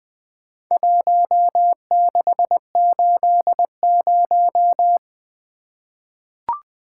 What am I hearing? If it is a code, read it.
Morse code, heard as a keyed tone, about 20 wpm, 700 Hz. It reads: 1680
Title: Four Letter-Number Combinations - 20wpm